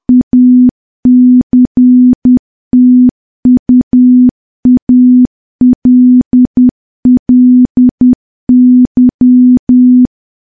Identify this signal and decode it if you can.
Morse code, heard as a keyed tone, about 10 wpm, 250 Hz. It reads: ACTUALLY
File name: morse / morse-training-10WPM